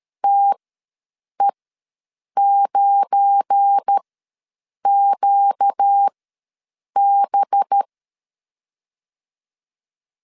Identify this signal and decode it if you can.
Morse code, heard as a keyed tone, about 13 wpm, spaced as for 7 wpm, 800 Hz. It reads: TE9QB
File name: LEZIONI / 8 YJ45 / 8 YJ45 ws 12wpm 3146